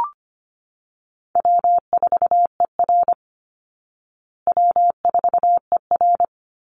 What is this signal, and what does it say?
Morse code, heard as a keyed tone, about 25 wpm, 700 Hz. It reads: W4ER W4ER
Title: Easy Call Signs Repeated 2x - 25wpm